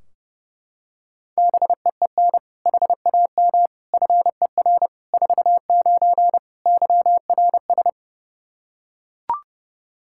Morse code, heard as a keyed tone, about 30 wpm, 700 Hz. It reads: BEEN HAM FER 49 YRS
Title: K1USN SST QSO Practice 30wpm